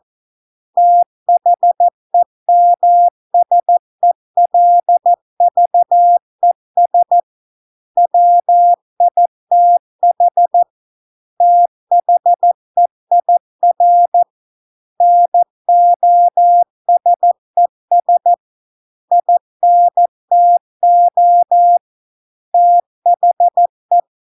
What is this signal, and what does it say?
Morse code, heard as a keyed tone, about 14 wpm, 700 Hz. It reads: THEMSELVES WITH THEIR NOSES INTO THE